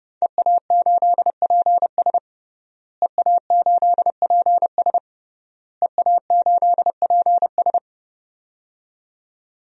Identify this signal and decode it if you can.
Morse code, heard as a keyed tone, about 30 wpm, 700 Hz. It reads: EA8PS EA8PS EA8PS
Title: DXCC-100 Call Signs - 5 Characters - Repeated 3x - 30wpm